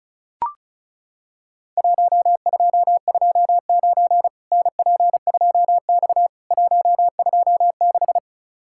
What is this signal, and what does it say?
Morse code, heard as a keyed tone, about 36 wpm, 700 Hz. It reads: 1229 NP2X 126
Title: Worked All Europe DX Contest - QTC Exchange - 35wpm